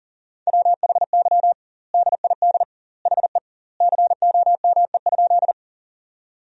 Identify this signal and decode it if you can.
Morse code, heard as a keyed tone, about 40 wpm, 700 Hz. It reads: WHY DID HE COME?
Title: 2x Word-Spacing - Sentences from Top 200 Words - 40wpm